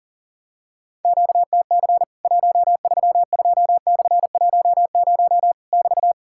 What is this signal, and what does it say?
Morse code, heard as a keyed tone, about 40 wpm, 700 Hz. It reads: QTC 132/10 <BT>
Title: Worked All Europe DX Contest - QTC Exchange - 40wpm